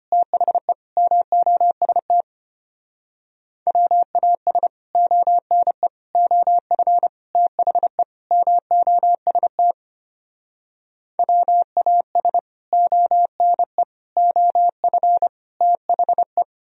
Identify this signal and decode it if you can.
Morse code, heard as a keyed tone, about 33 wpm, 700 Hz. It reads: THE MOST WAS ONE OF THE MOST WAS ONE OF THE